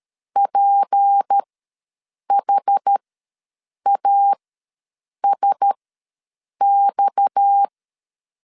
Morse code, heard as a keyed tone, about 13 wpm, spaced as for 7 wpm, 800 Hz. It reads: PHASX
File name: LEZIONI / 6 PKX0 / 6 PKX0 ws 12wpm 3448